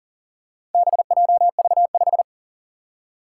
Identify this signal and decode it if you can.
Morse code, heard as a keyed tone, about 40 wpm, 700 Hz. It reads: BJV5
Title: Four Challenging Letter-Number Combinations 40wpm